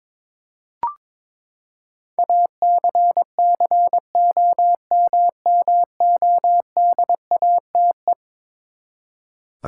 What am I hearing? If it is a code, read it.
Morse code, heard as a keyed tone, about 22 wpm, 700 Hz. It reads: ACCOMMODATE
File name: practice / 22wpm / Eleven Letter Words - 22wpm